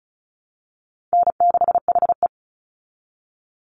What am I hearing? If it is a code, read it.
Morse code, heard as a keyed tone, about 34 wpm, 700 Hz. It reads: N6HE
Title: Call Signs Repeated 3x - 35wpm